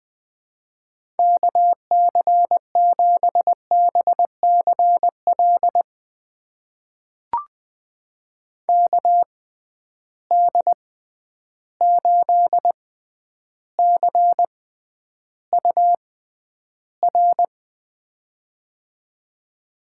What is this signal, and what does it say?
Morse code, heard as a keyed tone, about 20 wpm, spaced as for 10 wpm, 700 Hz. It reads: KC7BCL KD8CUR
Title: USA Call Signs - 6 Characters - 6x Character Spacing - 20wpm